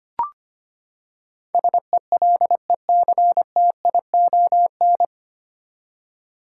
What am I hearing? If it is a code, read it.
Morse code, heard as a keyed tone, about 25 wpm, 700 Hz. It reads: SELECTION
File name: Nine Letter Words - 25wpm